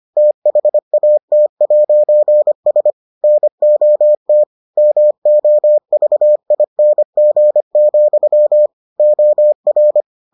Morse code, heard as a keyed tone, about 25 wpm, 600 Hz. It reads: THAT'S NOT MOVING, OR